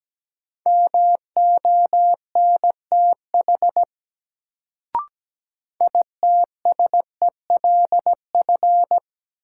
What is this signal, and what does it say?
Morse code, heard as a keyed tone, about 17 wpm, 700 Hz. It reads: MONTH ITSELF
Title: Sentences from Top 800 Words - New Word Drill - 17wpm